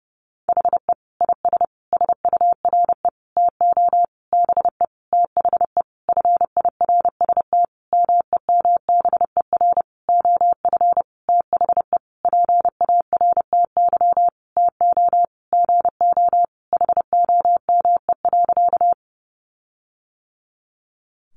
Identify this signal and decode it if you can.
Morse code, heard as a keyed tone, about 30 wpm, 700 Hz. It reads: HE IS SURE TO BE THE FIRST MEMBER OF THE PARTY TO GO HOME.
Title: Sentences from Top 1400 Words 30wpm